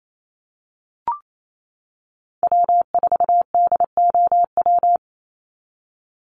Morse code, heard as a keyed tone, about 28 wpm, 700 Hz. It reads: W4DOW
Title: NA SOTA Chaser Callsigns 28wpm